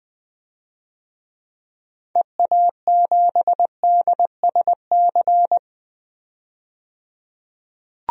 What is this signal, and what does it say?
Morse code, heard as a keyed tone, about 20 wpm, 700 Hz. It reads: EA7DSC
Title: DXCC-100 Call Signs - 6 Characters - 20wpm